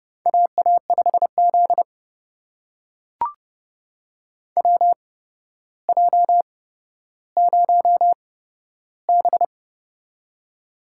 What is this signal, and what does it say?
Morse code, heard as a keyed tone, about 30 wpm, spaced as for 13 wpm, 700 Hz. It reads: AA5Z WJ0B
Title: USA Call Signs - 4 Characters - 8x Character Spacing - 30wpm